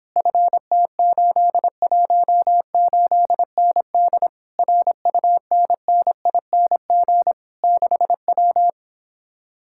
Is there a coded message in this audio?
Yes: Morse code from 0.2 to 8.7 s, heard as a keyed tone, about 26 wpm, 700 Hz. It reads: FT818ND RUNNING 6W